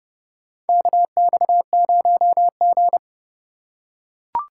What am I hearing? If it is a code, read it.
Morse code, heard as a keyed tone, about 30 wpm, 700 Hz. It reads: KX0G
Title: USA Call Signs - 4 Characters - 6x Character Spacing - 30wpm